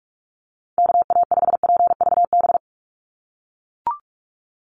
Morse code, heard as a keyed tone, about 44 wpm, 690 Hz. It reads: KA5PVB
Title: NA SOTA Chaser Callsigns 45wpm